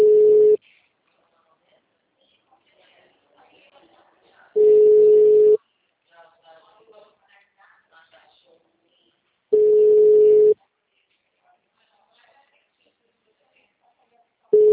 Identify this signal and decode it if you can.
Ringback tone (Europe, 425 Hz), with speech.
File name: out- -2010- - - .30610